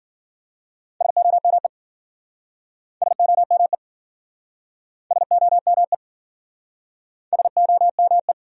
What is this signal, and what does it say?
Morse code, heard as a keyed tone, about 52 wpm, 700 Hz. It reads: SOME SOME SOME SOME